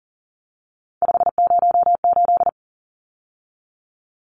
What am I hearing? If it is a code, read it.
Morse code, heard as a keyed tone, about 40 wpm, 700 Hz. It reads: 508
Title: Numbers - 3-Digit 40wpm